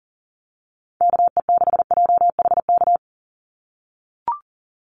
Morse code, heard as a keyed tone, about 40 wpm, 700 Hz. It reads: KE6JHK
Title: USA Call Signs 40wpm